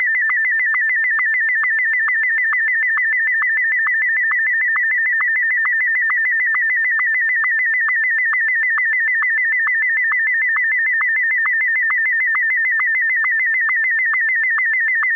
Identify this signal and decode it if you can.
SSTV (slow-scan television), Martin 1.